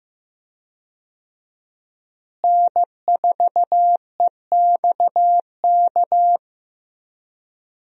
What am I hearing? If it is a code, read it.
Morse code, heard as a keyed tone, about 15 wpm, 700 Hz. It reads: N4EXK
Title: USA Call Signs - 5 Characters - Repeated 2x - 15wpm